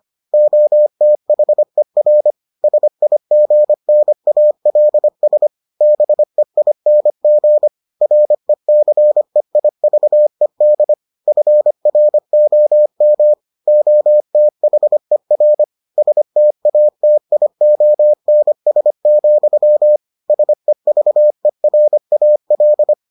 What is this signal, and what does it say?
Morse code, heard as a keyed tone, about 25 wpm, 600 Hz. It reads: OTHER SIGNALS BEING RECEIVED FROM OTHER STATIONS, SEVERAL